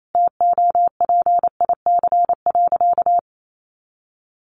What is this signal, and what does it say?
Morse code, heard as a keyed tone, about 28 wpm, 700 Hz. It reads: TOPIC.
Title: Sentences from Top 1200 Words 28wpm